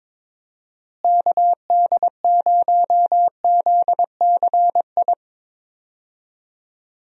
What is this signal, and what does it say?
Morse code, heard as a keyed tone, about 21 wpm, 700 Hz. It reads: KD0ZCI
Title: USA Call Signs - 6 Characters - Repeated 3x - 22wpm